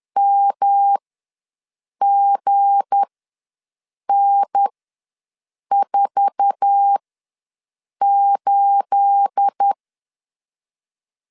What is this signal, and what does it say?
Morse code, heard as a keyed tone, about 10 wpm, spaced as for 6 wpm, 800 Hz. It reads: MGN48
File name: LEZIONI / 9 28 ws 10wpm 4124